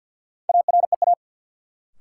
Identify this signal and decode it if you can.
Morse code, heard as a keyed tone, about 52 wpm, 700 Hz. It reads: AREA
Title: Sentences from Top 1000 Words - Review Part 2 - New Word Drill - 50wpm